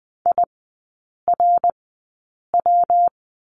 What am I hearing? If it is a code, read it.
Morse code, heard as a keyed tone, about 20 wpm, spaced as for 8 wpm, 700 Hz. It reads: IRW
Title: 159 Lesson 26e - Letter g - Three Characters 20wpm